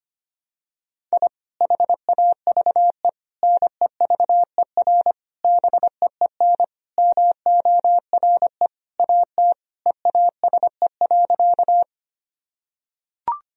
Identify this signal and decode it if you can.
Morse code, heard as a keyed tone, about 25 wpm, 700 Hz. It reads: I HAVE NEVER BEEN MORE AT EASE.
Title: Sentences from Top 1000 Words - Review Part 1 - 25wpm